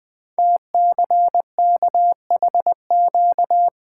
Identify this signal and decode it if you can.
Morse code, heard as a keyed tone, about 20 wpm, 700 Hz. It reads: TCKHQ
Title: Code Groups - 5 Letters - 20wpm